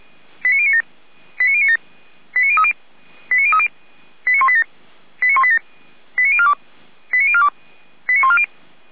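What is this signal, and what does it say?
Five-tone selective call (ZVEI), heard as quick runs of five tones, tones of about 70 ms. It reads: 89097 89097 89030 89030 89187 89187 89042 89042 89140
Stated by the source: Five-tone ZVEI selective call received on an SDR.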